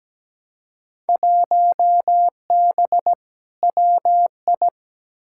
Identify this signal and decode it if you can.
Morse code, heard as a keyed tone, about 17 wpm, 700 Hz. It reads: 1B WI
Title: Field Day Exchanges Encore - 17wpm